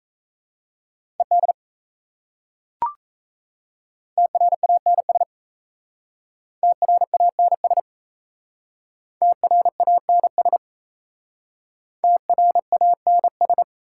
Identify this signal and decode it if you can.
Morse code, heard as a keyed tone, about 37 wpm, 700 Hz. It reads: ED TRANS TRANS TRANS TRANS